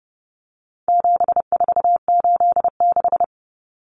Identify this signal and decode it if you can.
Morse code, heard as a keyed tone, about 30 wpm, 700 Hz. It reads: 7486
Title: Common Cut Numbers - 4-Digit 30wpm